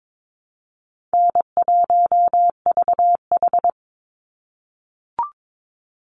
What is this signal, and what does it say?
Morse code, heard as a keyed tone, about 22 wpm, 700 Hz. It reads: N1VH
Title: Easy Call Signs - 22wpm